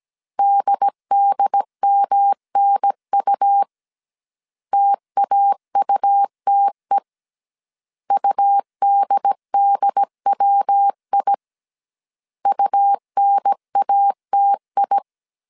Morse code, heard as a keyed tone, about 17 wpm, 800 Hz. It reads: DDMNU TAUTE UDDWI UNATI